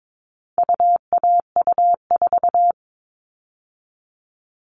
Morse code, heard as a keyed tone, about 22 wpm, 700 Hz. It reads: UAU4
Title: All Cut Numbers - 4-Digit 22wpm